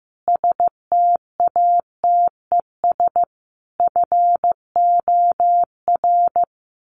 Morse code, heard as a keyed tone, about 15 wpm, 700 Hz. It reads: STATES FOR
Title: Sets of 5 Words Encore2 15wpm